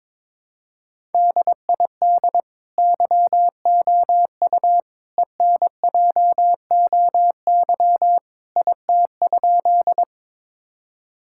Morse code, heard as a keyed tone, about 22 wpm, 700 Hz. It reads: DID YOU ENJOY IT?
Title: Sentences from Top 1000 Words - Review Part 5 - 22wpm